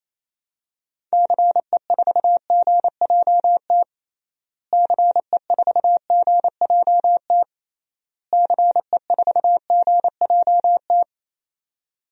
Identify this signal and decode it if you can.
Morse code, heard as a keyed tone, about 28 wpm, 700 Hz. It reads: CE4GJT CE4GJT CE4GJT
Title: DXCC-100 Call Signs - 6 Characters - Repeated 3x - 28wpm